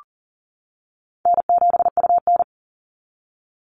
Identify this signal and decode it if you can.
Morse code, heard as a keyed tone, about 38 wpm, 700 Hz. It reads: N7UN
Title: NA SOTA Activator Callsigns 40wpm